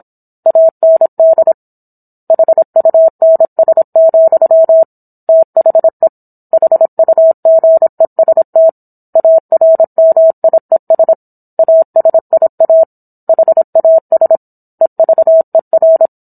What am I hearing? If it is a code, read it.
Morse code, heard as a keyed tone, about 27 wpm, 650 Hz. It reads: AND HUNS, THE HUGEST ARMIES ASIA HAS EVER
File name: Book One / 17-Chapter 17 - 26 WPM